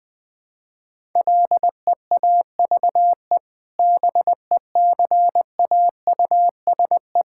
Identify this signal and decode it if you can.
Morse code, heard as a keyed tone, about 20 wpm, 700 Hz. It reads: LEAVE BECAUSE
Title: Sentences with 2-Word Phrases - 20wpm